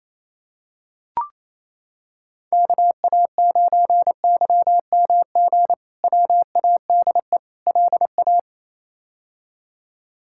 Morse code, heard as a keyed tone, about 28 wpm, 700 Hz. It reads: KA9YMG WADE LA
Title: K1USN SST QSO Practice - 28wpm